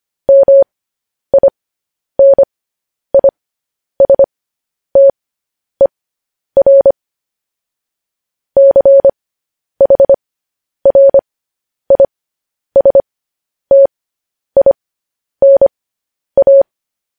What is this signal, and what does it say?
Morse code, heard as a keyed tone, about 25 wpm, spaced as for 10 wpm, 550 Hz. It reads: MINISTER CHRISTINA
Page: news-headlines-in-morse-code-at-10-wpm